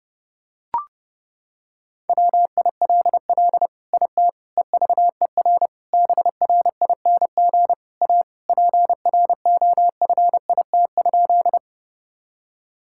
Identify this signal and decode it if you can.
Morse code, heard as a keyed tone, about 30 wpm, 700 Hz. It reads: WILL IT EVER BRING A PROFIT?
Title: Sentences from Top 1000 Words - Review Part 5 - 30wpm